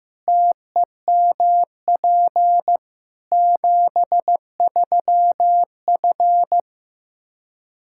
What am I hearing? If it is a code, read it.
Morse code, heard as a keyed tone, about 15 wpm, 700 Hz. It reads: TEMP 73F
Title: Practice QSO - Exchange 2 - Info - 15wpm